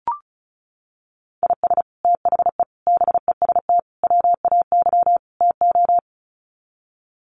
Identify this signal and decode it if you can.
Morse code, heard as a keyed tone, about 34 wpm, 700 Hz. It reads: IS THE BEST WAY TO
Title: Sets of 5 Words Encore 35wpm